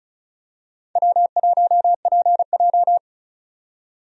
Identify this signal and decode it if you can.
Morse code, heard as a keyed tone, about 34 wpm, 700 Hz. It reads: W1PJ
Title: USA Call Signs - 4 Characters - 35wpm